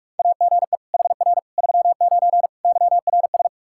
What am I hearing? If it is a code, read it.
Morse code, heard as a keyed tone, about 46 wpm, 700 Hz. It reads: AGE HR 39 YRS
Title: Practice QSO - Exchange 3 - More Info - 45wpm